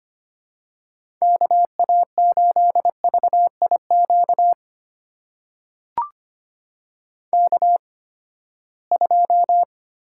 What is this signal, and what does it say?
Morse code, heard as a keyed tone, about 25 wpm, spaced as for 17 wpm, 700 Hz. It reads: KA8VIQ K2
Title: USA Call Signs - 8x Character Spacing - 25wpm